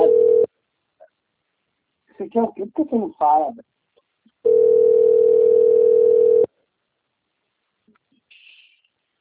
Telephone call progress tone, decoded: ringback tone

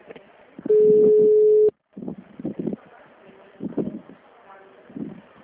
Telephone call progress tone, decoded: ringback tone

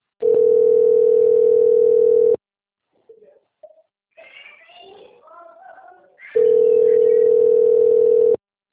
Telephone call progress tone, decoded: ringback tone

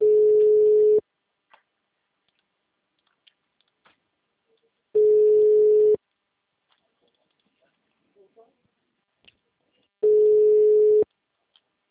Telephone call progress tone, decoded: ringback tone

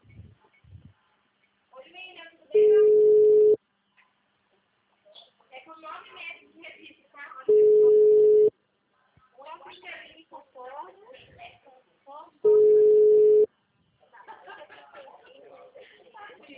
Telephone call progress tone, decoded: ringback tone

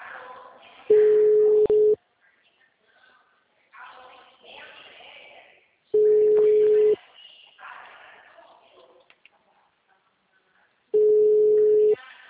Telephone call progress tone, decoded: ringback tone

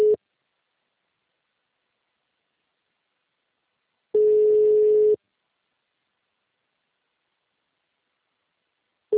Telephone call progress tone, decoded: ringback tone

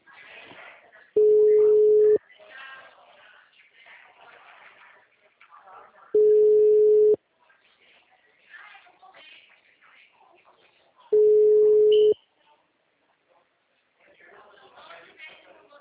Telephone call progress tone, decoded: ringback tone

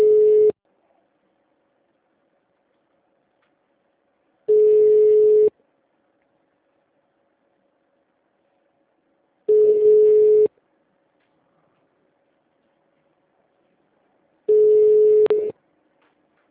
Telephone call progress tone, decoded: ringback tone

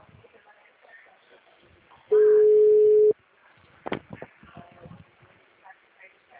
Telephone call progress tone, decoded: ringback tone

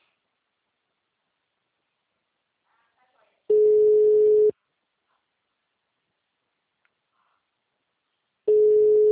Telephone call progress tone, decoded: ringback tone

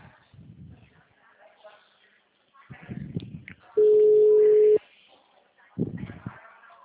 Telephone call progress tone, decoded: ringback tone

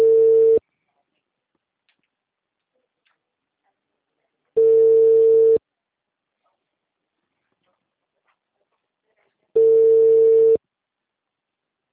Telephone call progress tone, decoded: ringback tone

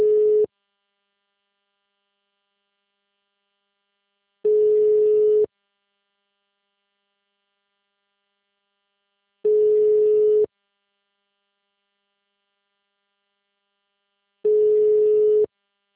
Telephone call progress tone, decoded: ringback tone

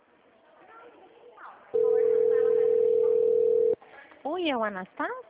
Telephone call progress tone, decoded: ringback tone

